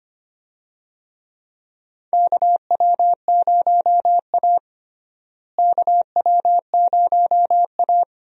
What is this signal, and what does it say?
Morse code, heard as a keyed tone, about 25 wpm, 700 Hz. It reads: KW0A KW0A
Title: USA Call Signs - 4 Characters - Repeated 3x - 25wpm